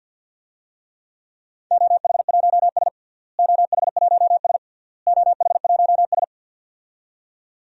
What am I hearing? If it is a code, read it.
Morse code, heard as a keyed tone, about 46 wpm, 700 Hz. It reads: OH1S OH1S OH1S
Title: DXCC-100 Call Signs - 4 Characters - Repeated 3x - 50wpm